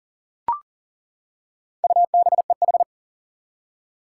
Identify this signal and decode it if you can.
Morse code, heard as a keyed tone, about 40 wpm, 700 Hz. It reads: UBEH
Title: Four Challenging Letter-Number Combinations 40wpm